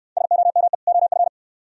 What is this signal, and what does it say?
Morse code, heard as a keyed tone, about 69 wpm, 700 Hz. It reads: SOME OF